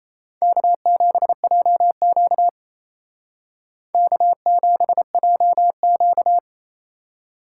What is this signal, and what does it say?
Morse code, heard as a keyed tone, about 36 wpm, 700 Hz. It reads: K7JQ K7JQ